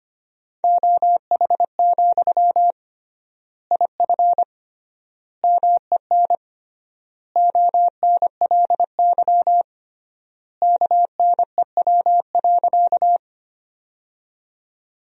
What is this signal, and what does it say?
Morse code, heard as a keyed tone, about 25 wpm, 700 Hz. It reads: OH, IF MEN ONLY KNEW.
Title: 3x Word-Spacing - Sentences from Top 500 Words - 25wpm